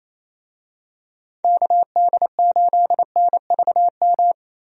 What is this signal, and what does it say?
Morse code, heard as a keyed tone, about 28 wpm, 700 Hz. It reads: KD8NVM
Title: USA Call Signs - 6 Characters - Repeated 2x - 28wpm